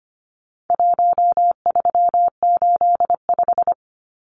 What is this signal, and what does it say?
Morse code, heard as a keyed tone, about 25 wpm, 700 Hz. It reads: 1385
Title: Numbers - 4-Digit 25wpm